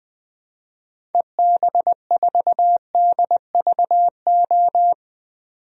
Morse code, heard as a keyed tone, about 20 wpm, 700 Hz. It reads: EB4DVO
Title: DXCC-100 Call Signs - 6 Characters - 20wpm